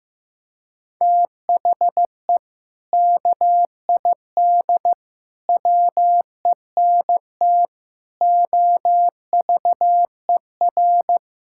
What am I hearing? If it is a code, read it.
Morse code, heard as a keyed tone, about 15 wpm, 700 Hz. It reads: THE KID WENT OVER
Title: Sentences from Top 1600 Words 15wpm